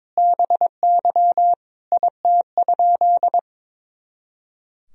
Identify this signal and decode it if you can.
Morse code, heard as a keyed tone, about 22 wpm, 700 Hz. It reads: BY IT?
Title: Sentences from Top 100 Words and Abbreviations 22wpm